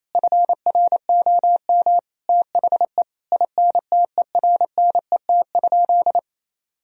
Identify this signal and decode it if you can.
Morse code, heard as a keyed tone, about 28 wpm, 700 Hz. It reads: FROM THE INTERNET?
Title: Sentences from Top 1000 Words - Review Part 4 - 28wpm